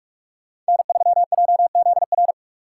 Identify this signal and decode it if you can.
Morse code, heard as a keyed tone, about 48 wpm, 700 Hz. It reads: N3JZR
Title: Speed Racing - USA 1x3 Call Signs 45wpm